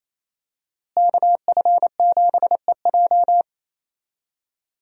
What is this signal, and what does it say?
Morse code, heard as a keyed tone, about 28 wpm, 700 Hz. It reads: KF7EJ
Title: USA Call Signs - 5 Characters - Repeated 3x - 28wpm